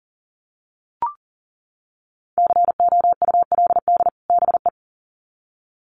Morse code, heard as a keyed tone, about 40 wpm, 700 Hz. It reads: COULD BE
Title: Sets of 2 Words 40wpm